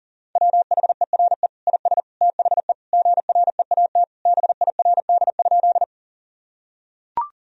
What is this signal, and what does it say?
Morse code, heard as a keyed tone, about 40 wpm, 700 Hz. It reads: WHERE IS THE GREAT BIRD?
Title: Sentences from Top 1000 Words - Review Part 2 - 40wpm